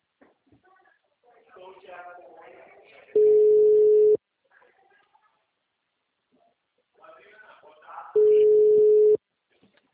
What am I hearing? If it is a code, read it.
Ringback tone (Europe, 425 Hz), with speech.